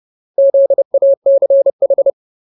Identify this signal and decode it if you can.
Morse code, heard as a keyed tone, about 30 wpm, 550 Hz. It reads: ZACH
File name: Ultimate Word List / 30WPM Ultimate